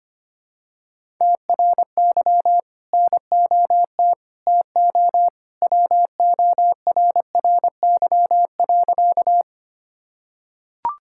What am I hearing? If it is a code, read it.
Morse code, heard as a keyed tone, about 25 wpm, 700 Hz. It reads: TRY NOT TO WORRY.
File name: practice / 25wpm / Sentences with 2-Word Phrases - 25wpm